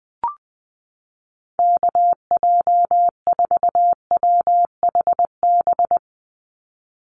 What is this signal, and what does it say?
Morse code, heard as a keyed tone, about 19 wpm, 700 Hz. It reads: KJ4WHB
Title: USA Call Signs 20wpm